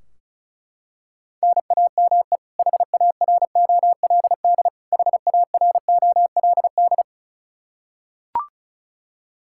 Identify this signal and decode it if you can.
Morse code, heard as a keyed tone, about 36 wpm, 700 Hz. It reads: NAME HAROLD HAROLD
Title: K1USN SST QSO Practice 35wpm